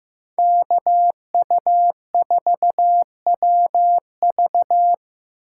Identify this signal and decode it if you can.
Morse code, heard as a keyed tone, about 15 wpm, 700 Hz. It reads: KU4WV
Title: USA Call Signs - 5 Characters - 4x Character Spacing - 15wpm